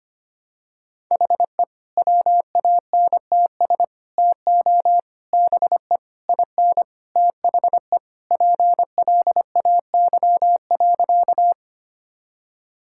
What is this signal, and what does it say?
Morse code, heard as a keyed tone, about 25 wpm, 700 Hz. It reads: HE WANTS TO BE IN THE PLAY.
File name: practice / 25wpm / Sentences with 4-Word Phrases - 25wpm